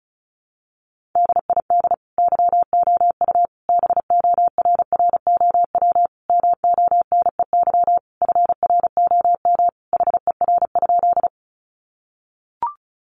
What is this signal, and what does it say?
Morse code, heard as a keyed tone, about 34 wpm, 700 Hz. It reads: DID YOU BORROW MONEY FROM HER?
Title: Sentences from Top 1500 Words 35wpm